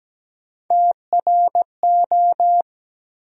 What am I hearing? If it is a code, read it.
Morse code, heard as a keyed tone, about 17 wpm, 700 Hz. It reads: TRO
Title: USA Call Sign Postfixes - 3 Letters 17wpm